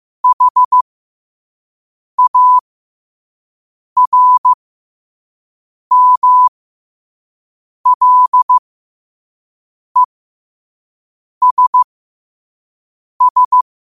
Morse code, heard as a keyed tone, about 15 wpm, spaced as for 5 wpm, 1000 Hz. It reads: HARMLESS
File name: qotdfiles / 5wpm